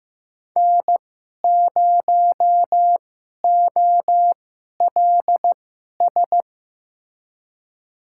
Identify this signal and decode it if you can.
Morse code, heard as a keyed tone, about 15 wpm, spaced as for 11 wpm, 700 Hz. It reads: N0OLS
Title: USA Call Signs - 5 Characters - 2x Character Spacing - 15wpm